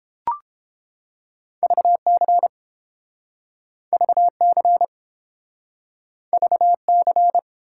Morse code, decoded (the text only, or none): VC VC VC